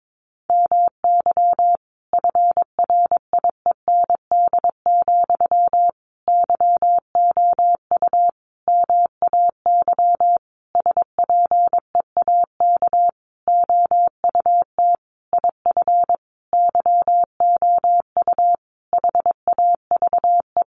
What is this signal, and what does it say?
Morse code, heard as a keyed tone, about 22 wpm, 700 Hz. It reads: MY FRIEND, YOU MAY SPEAK OUT IF YOU HAVE
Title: Sentences from Top 1300 Words 22wpm